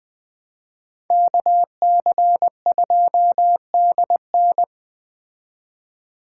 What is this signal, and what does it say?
Morse code, heard as a keyed tone, about 20 wpm, 700 Hz. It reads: KC2DN
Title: Speed Racing - USA 2x2 Call Signs 17wpm